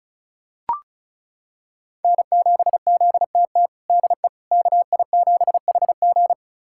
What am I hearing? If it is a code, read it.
Morse code, heard as a keyed tone, about 36 wpm, 700 Hz. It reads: N7ZTT DE KI7HG
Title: USA Callsign DE Callsign - 5 Characters - 35wpm